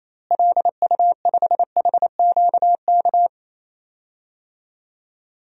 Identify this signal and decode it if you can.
Morse code, heard as a keyed tone, about 28 wpm, 700 Hz. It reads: LU5HQK
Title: DXCC-100 Call Signs - 6 Characters - Repeated 2x - 28wpm